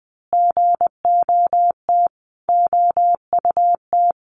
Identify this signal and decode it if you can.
Morse code, heard as a keyed tone, about 20 wpm, 700 Hz. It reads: GOT OUT
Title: A Sound Of Thunder 20wpm